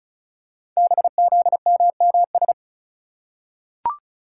Morse code, heard as a keyed tone, about 36 wpm, 700 Hz. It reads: BZMMS
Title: Code Groups - 5 Letters - 35wpm